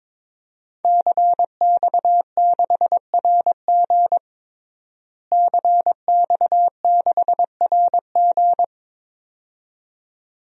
Morse code, heard as a keyed tone, about 22 wpm, 700 Hz. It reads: CX6RG CX6RG
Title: DXCC-100 Call Signs - 5 Characters - Repeated 2x - 22wpm